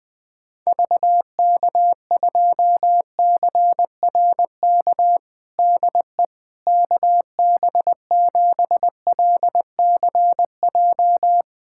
Morse code, heard as a keyed tone, about 20 wpm, 700 Hz. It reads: VK2CRK DE KB7LCJ
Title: USA Callsign DE Callsign 20wpm